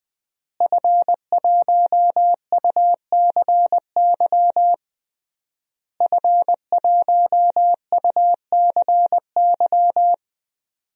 Morse code, heard as a keyed tone, about 21 wpm, 700 Hz. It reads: F1UCY F1UCY
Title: DXCC-100 Call Signs - 5 Characters - Repeated 3x - 20wpm